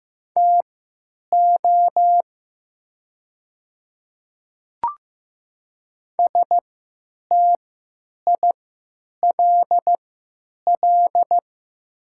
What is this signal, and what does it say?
Morse code, heard as a keyed tone, about 15 wpm, spaced as for 9 wpm, 700 Hz. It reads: TO STILL
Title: 3x Character Spacing - Top 500 Words - 15wpm